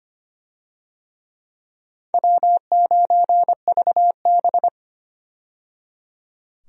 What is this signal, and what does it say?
Morse code, heard as a keyed tone, about 25 wpm, 700 Hz. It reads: W9VB
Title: USA Call Signs - 4 Characters - 25wpm